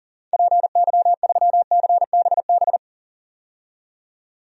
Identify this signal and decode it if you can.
Morse code, heard as a keyed tone, about 38 wpm, 700 Hz. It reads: PY3CBB
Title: DXCC-100 Call Signs - 6 Characters - 40wpm